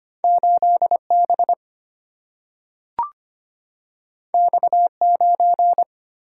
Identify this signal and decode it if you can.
Morse code, heard as a keyed tone, about 25 wpm, 700 Hz. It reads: 8B X9
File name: Two Letter-Number Combinations - 25wpm